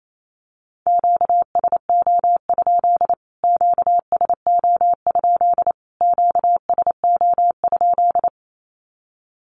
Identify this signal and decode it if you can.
Morse code, heard as a keyed tone, about 28 wpm, 700 Hz. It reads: QSO? QSO? QSO?
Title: Q-Codes Repeated 3x 28wpm